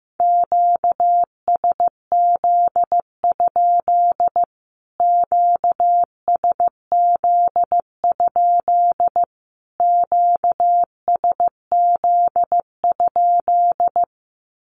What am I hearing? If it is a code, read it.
Morse code, heard as a keyed tone, about 15 wpm, 700 Hz. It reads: QSZ? QSZ? QSZ?
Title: Q-Codes Repeated 3x 15wpm